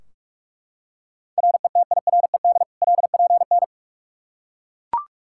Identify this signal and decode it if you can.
Morse code, heard as a keyed tone, about 41 wpm, 700 Hz. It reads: RETIRED LPN
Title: Practice QSO - Exchange 3 - More Info - 45wpm